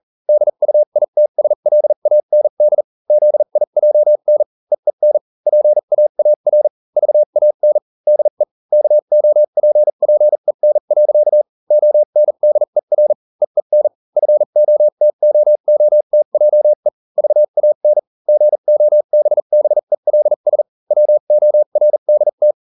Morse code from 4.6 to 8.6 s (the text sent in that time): EEN PAAR VAN DE